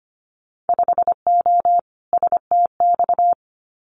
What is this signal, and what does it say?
Morse code, heard as a keyed tone, about 25 wpm, 700 Hz. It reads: 5O STX
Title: Winter Field Day Exchanges 25wpm